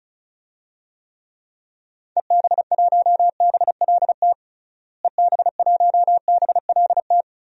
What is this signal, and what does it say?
Morse code, heard as a keyed tone, about 36 wpm, 700 Hz. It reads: EB1BLT EB1BLT
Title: DXCC-100 Call Signs - 6 Characters - Repeated 3x - 35wpm